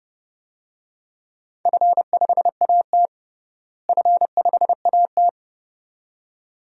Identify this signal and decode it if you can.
Morse code, heard as a keyed tone, about 30 wpm, 700 Hz. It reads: F5AT F5AT
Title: DXCC-100 Call Signs - 4 Characters - Repeated 2x - 30wpm